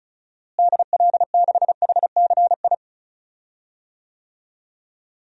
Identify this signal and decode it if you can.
Morse code, heard as a keyed tone, about 34 wpm, 700 Hz. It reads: DL6HCI
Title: DXCC-100 Call Signs - 6 Characters - Repeated 3x - 35wpm